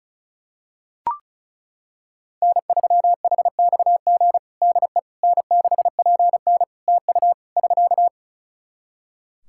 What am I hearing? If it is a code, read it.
Morse code, heard as a keyed tone, about 36 wpm, 700 Hz. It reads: N3HXG DE N6PN TU <SK>
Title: K1USN SST QSO Practice 35wpm